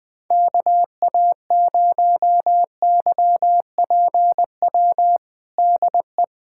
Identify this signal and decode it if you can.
Morse code, heard as a keyed tone, about 20 wpm, 700 Hz. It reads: KA0YPW DE
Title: USA Callsign DE Callsign - 6 Characters - 20wpm